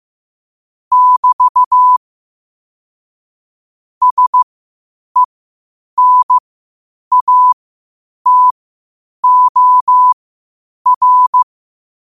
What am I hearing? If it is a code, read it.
Morse code, heard as a keyed tone, about 15 wpm, spaced as for 8 wpm, 1000 Hz. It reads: <BT> SENATOR
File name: qotdfiles / 8wpm